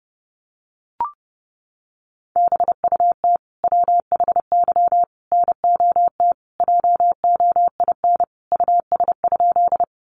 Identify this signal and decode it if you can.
Morse code, heard as a keyed tone, about 30 wpm, 700 Hz. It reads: BUT WHY NOT JOIN US?
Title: Sentences from Top 1400 Words 30wpm